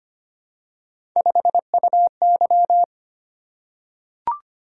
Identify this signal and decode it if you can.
Morse code, heard as a keyed tone, about 25 wpm, 700 Hz. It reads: 5UY
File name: Three Letter-Number Combinations - 25wpm